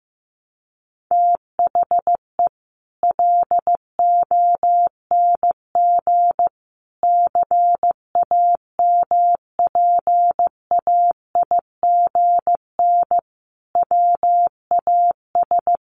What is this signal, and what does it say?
Morse code, heard as a keyed tone, about 15 wpm, 700 Hz. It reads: THE LONG CAMPAIGN WAS
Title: Sentences from Top 1200 Words 15wpm